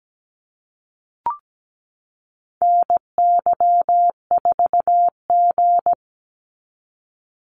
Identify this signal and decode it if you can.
Morse code, heard as a keyed tone, about 17 wpm, 700 Hz. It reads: NY4G
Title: NA SOTA Activator Callsigns 17wpm